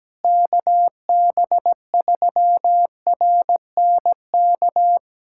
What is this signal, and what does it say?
Morse code, heard as a keyed tone, about 17 wpm, 700 Hz. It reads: KB3RNK
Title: USA Callsign DE Callsign 17wpm